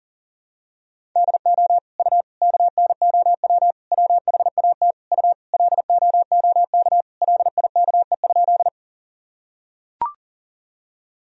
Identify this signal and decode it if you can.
Morse code, heard as a keyed tone, about 40 wpm, 700 Hz. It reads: DO U KNOW WHAT U LOOK LIKE?
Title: Sentences from Top 100 Words and Abbreviations 40wpm